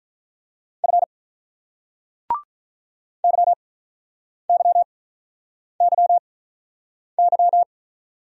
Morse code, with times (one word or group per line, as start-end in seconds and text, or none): 0.8-1.0 F
3.2-7.6 YYYY